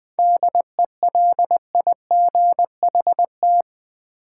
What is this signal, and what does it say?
Morse code, heard as a keyed tone, about 20 wpm, 690 Hz. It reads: DELIGHT
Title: 7 Letter Words - First 3 Letters Given - 20wpm